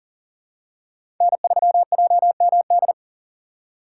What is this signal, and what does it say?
Morse code, heard as a keyed tone, about 40 wpm, 700 Hz. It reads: N3JMD
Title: USA Call Signs - 5 Characters - 3x Character Spacing - 40wpm